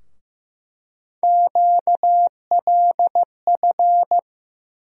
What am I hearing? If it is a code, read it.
Morse code, heard as a keyed tone, about 15 wpm, 700 Hz. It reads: QLF
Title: USA Call Sign Postfixes - 3 Letters 15wpm